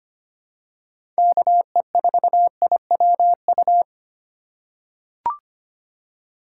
Morse code, heard as a keyed tone, about 25 wpm, 700 Hz. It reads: KE4IWU
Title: USA Call Signs - 6 Characters - 2x Character Spacing - 25wpm